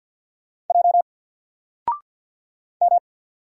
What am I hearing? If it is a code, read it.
Morse code, heard as a keyed tone, about 49 wpm, 700 Hz. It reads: J M
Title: Single Character (Letter:Number:Punctuation:Prosign) 50wpm